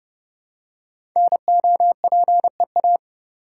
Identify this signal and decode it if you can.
Morse code, heard as a keyed tone, about 28 wpm, 700 Hz. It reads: NOPEA
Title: Code Groups - 5 Letters - 30wpm